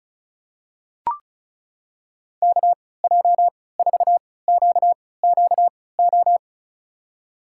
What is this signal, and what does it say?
Morse code, heard as a keyed tone, about 34 wpm, spaced as for 20 wpm, 700 Hz. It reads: KJ4QQO
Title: USA Call Signs - 3x Character Spacing - 35wpm